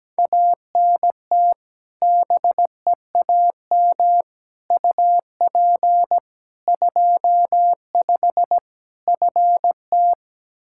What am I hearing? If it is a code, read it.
Morse code, heard as a keyed tone, about 17 wpm, 700 Hz. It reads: ANT BEAM UP 25 FT
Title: K1USN SST QSO Practice 17wpm